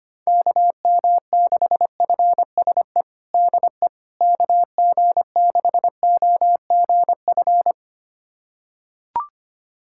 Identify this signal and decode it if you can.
Morse code, heard as a keyed tone, about 25 wpm, 700 Hz. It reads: KM6FSE DE KG6OGF
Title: USA Callsign DE Callsign - 6 Characters - 25wpm